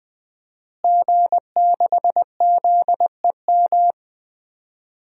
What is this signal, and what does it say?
Morse code, heard as a keyed tone, about 20 wpm, 700 Hz. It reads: G6ZEM
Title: DXCC-100 Call Signs - 5 Characters - Repeated 3x - 20wpm